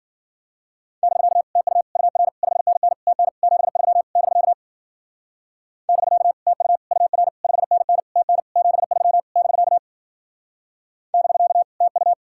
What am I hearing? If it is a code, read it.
Morse code, heard as a keyed tone, about 49 wpm, 700 Hz. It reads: <BK> TU UR 5NN TN 73 <BK> <BK> TU UR 5NN TN 73 <BK> <BK> TU
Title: Speed-Racing Common QSO Phrases - 40wpm